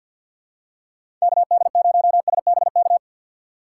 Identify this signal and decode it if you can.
Morse code, heard as a keyed tone, about 52 wpm, 700 Hz. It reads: KD0SBK
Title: USA Call Signs - 6 Characters - 3x Character Spacing - 50wpm